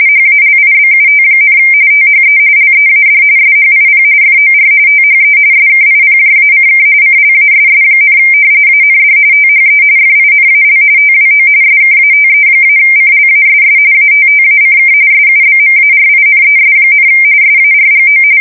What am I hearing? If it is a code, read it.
RTTY, about 45.45 baud. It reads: HALIBUT, THE LEAD SHIP FOR THE SKIPJACK CLASS, AND THE FINAL TWO SKATE-CLASS ATTACK SUBMARINES, SARGO
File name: fsk170 / usnavy5